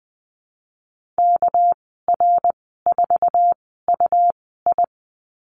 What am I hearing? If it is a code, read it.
Morse code, heard as a keyed tone, about 20 wpm, spaced as for 15 wpm, 700 Hz. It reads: KR4UI
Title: 207 Lesson 33g - Letter j - Call Signs 20wpm